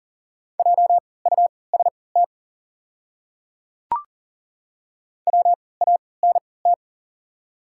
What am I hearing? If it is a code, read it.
Morse code, heard as a keyed tone, about 37 wpm, spaced as for 22 wpm, 700 Hz. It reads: JUST WANT
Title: 3x Character Spacing - Top 200 Words - 40wpm